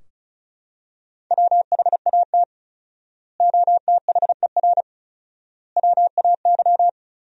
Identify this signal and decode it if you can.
Morse code, heard as a keyed tone, about 36 wpm, 700 Hz. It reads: WHAT OTHER WAY
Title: 4x Word-Spacing - Sentences from Top 100 Words - 35wpm